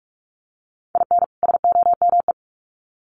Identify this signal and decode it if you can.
Morse code, heard as a keyed tone, about 44 wpm, 700 Hz. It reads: IN SOME
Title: Sets of 2 Words 45wpm